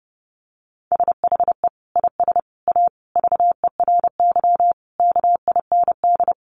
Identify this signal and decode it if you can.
Morse code, heard as a keyed tone, about 28 wpm, 700 Hz. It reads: SHE IS A VERY KIND
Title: Sentences from Top 1100 Words 30wpm